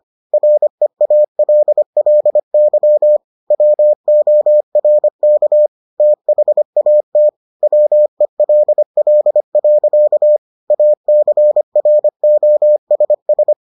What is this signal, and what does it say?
Morse code, heard as a keyed tone, about 25 wpm, 600 Hz. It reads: REALLY WORK THAT WELL. ACROSS